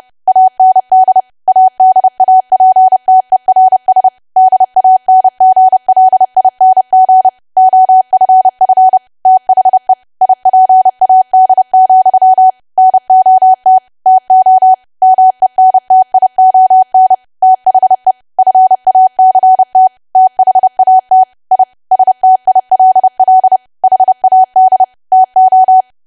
Morse code, heard as a keyed tone, about 30 wpm, 750 Hz. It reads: AND ADAPTERS DANGLING OFF THE IPAD, NOT TO MENTION THE FACT THAT I STILL HAD TO
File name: cwa / W1AW-Level3 / 30WPM